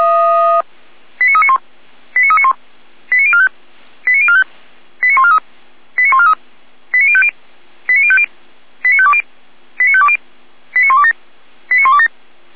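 Five-tone selective call (ZVEI), heard as quick runs of five tones, tones of about 70 ms. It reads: 89391 89391 89045 89045 89143 89143 89050 89050 89520 89520 89127 89127
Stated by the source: ZVEI five-tone call received on SDR, code 89126.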